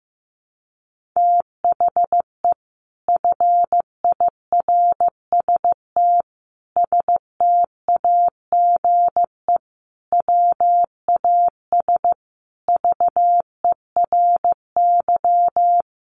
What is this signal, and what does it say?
Morse code, heard as a keyed tone, about 15 wpm, 700 Hz. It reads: THE FIRST STAGE WAS VERY
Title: Sentences from Top 1200 Words 15wpm